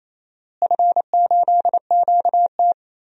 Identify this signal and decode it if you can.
Morse code, heard as a keyed tone, about 28 wpm, 700 Hz. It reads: F8QT
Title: DXCC-100 Call Signs - 4 Characters - Repeated 3x - 28wpm